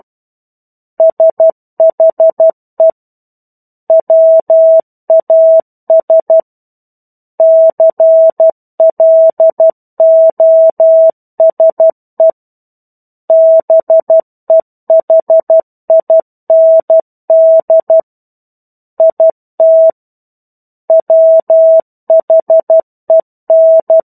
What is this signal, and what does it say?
Morse code, heard as a keyed tone, about 12 wpm, 650 Hz. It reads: SHE WAS CLOSE BEHIND IT WHEN
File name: 01-Chapter-01 12 WPM